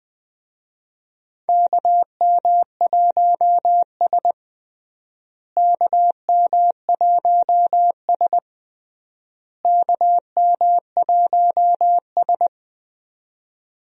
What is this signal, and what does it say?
Morse code, heard as a keyed tone, about 20 wpm, 700 Hz. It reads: KM1S KM1S KM1S
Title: USA Call Signs - 4 Characters - Repeated 3x - 20wpm